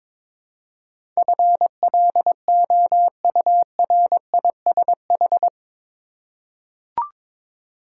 Morse code, heard as a keyed tone, about 22 wpm, 700 Hz. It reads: FLOURISH
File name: practice / 22wpm / Eight Letter Words - 22wpm